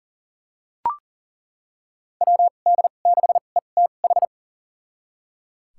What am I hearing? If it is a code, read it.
Morse code, heard as a keyed tone, about 38 wpm, spaced as for 28 wpm, 700 Hz. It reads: WD6ETH
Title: USA Call Signs - 2x Character Spacing - 40wpm